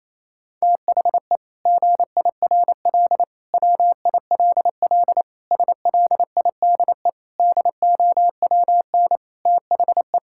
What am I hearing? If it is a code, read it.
Morse code, heard as a keyed tone, about 28 wpm, 700 Hz. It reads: THE GIRL WILL SLIDE DOWN THE
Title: Sentences from Top 1600 Words 28wpm